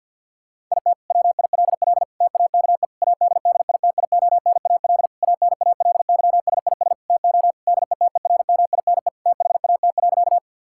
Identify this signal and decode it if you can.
Morse code, heard as a keyed tone, about 50 wpm, 700 Hz. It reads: IT WILL TAKE ADDITIONAL ANALYSIS TO DETERMINE THAT.
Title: Sentences from Top 1000 Words - Review Part 5 - 50wpm